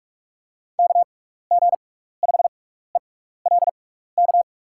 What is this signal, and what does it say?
Morse code, heard as a keyed tone, about 44 wpm, spaced as for 15 wpm, 700 Hz. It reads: KG5ELK
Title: USA Call Signs - 6 Characters - 6x Character Spacing - 45wpm